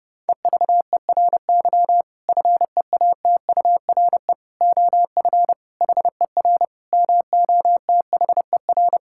Morse code, heard as a keyed tone, about 30 wpm, 700 Hz. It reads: EVERY FEATURE OF HER MOTHER
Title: Sentences from Top 1000 Words - Review Part 5 - 30wpm